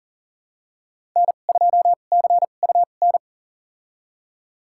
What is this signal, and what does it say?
Morse code, heard as a keyed tone, about 40 wpm, spaced as for 29 wpm, 700 Hz. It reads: N2CUN
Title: USA Call Signs - 2x Character Spacing - 40wpm